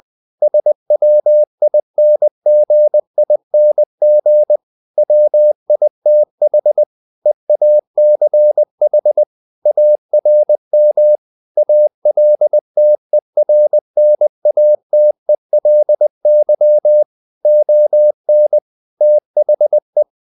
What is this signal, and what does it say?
Morse code, heard as a keyed tone, about 20 wpm, 600 Hz. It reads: SWINGING WITH EACH ARM ALTERNATELY ON THE
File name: the old man and the sea / the old man and the sea part 1 20wpm